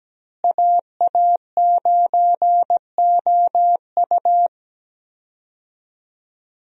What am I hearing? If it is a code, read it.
Morse code, heard as a keyed tone, about 17 wpm, 700 Hz. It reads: AA9OU
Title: USA Call Signs - 5 Characters - Repeated 3x - 17wpm